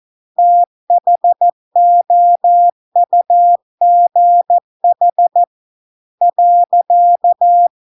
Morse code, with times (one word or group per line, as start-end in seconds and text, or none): 0.4-5.4 THOUGH
6.2-7.7 .